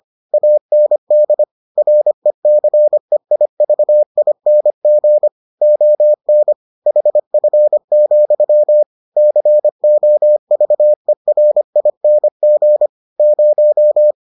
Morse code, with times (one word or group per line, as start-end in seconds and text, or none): 0.3-1.4 AND
1.8-5.3 RECEIVING
5.6-6.5 ON
6.9-8.8 HF,
9.2-12.9 COVERING
13.2-14.1 0